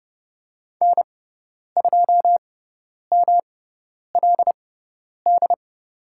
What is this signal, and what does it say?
Morse code, heard as a keyed tone, about 30 wpm, spaced as for 10 wpm, 700 Hz. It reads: N2MLD
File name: Lesson 18g - Letter m - Call Signs 30wpm